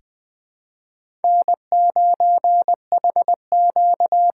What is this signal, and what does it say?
Morse code, heard as a keyed tone, about 20 wpm, 700 Hz. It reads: N9HQ
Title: USA Call Signs - 6x Character Spacing - 20wpm